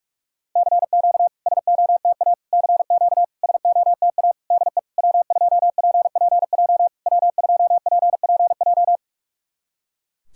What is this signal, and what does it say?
Morse code, heard as a keyed tone, about 46 wpm, 700 Hz. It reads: CQ SOTA CQ SOTA DE W2PPJ W2PPJ
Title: SOTA QSO Practice 45wpm